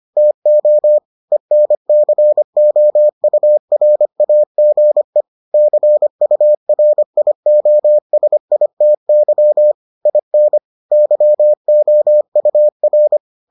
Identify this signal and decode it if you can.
Morse code, heard as a keyed tone, about 25 wpm, 600 Hz. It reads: TO ENCOURAGE CURIOSITY IN YOUR